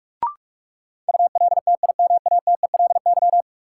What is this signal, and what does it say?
Morse code, heard as a keyed tone, about 46 wpm, 700 Hz. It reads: ULTIMATELY
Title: Sentences from Top 1200 Words - New Word Drill - 45wpm